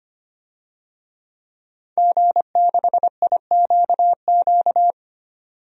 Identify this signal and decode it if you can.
Morse code, heard as a keyed tone, about 25 wpm, 700 Hz. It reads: G6IQQ
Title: DXCC-100 Call Signs - 5 Characters - 25wpm